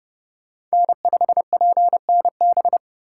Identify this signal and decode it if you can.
Morse code, heard as a keyed tone, about 31 wpm, 700 Hz. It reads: N5PNB